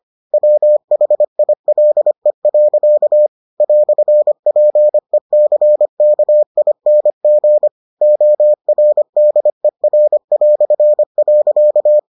Morse code, heard as a keyed tone, about 25 wpm, 600 Hz. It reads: WHILE. "PECKING ORDER".